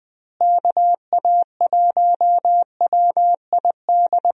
Morse code, heard as a keyed tone, about 20 wpm, 700 Hz. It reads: KA1WID
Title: USA Call Signs - 6 Characters - Repeated 3x - 20wpm